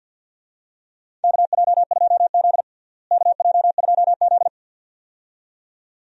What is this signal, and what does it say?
Morse code, heard as a keyed tone, about 53 wpm, 700 Hz. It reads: KJ2Z KJ2Z
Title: USA Call Signs - 4 Characters - Repeated 2x - 50wpm